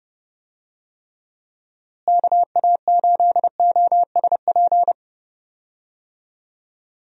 Morse code, heard as a keyed tone, about 30 wpm, 700 Hz. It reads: KA8OSP
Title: USA Call Signs - 6 Characters - Repeated 2x - 30wpm